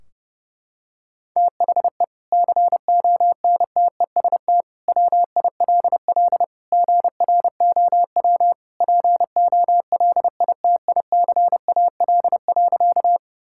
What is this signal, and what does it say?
Morse code, heard as a keyed tone, about 30 wpm, 700 Hz. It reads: THE CONTEST WILL GROW POLITICAL.
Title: Sentences from Top 1600 Words 30wpm